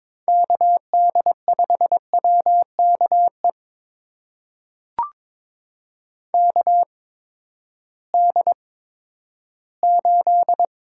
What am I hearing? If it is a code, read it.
Morse code, heard as a keyed tone, about 22 wpm, spaced as for 12 wpm, 700 Hz. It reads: KD5WKE KD8
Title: USA Call Signs - 6 Characters - 8x Character Spacing - 22wpm